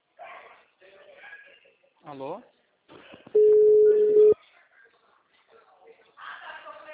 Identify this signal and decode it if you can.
Ringback tone (Europe, 425 Hz), with speech.